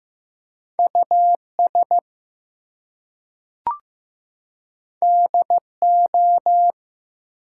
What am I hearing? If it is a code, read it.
Morse code, heard as a keyed tone, about 15 wpm, 700 Hz. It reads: US DO
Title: Two Letter Words - 15wpm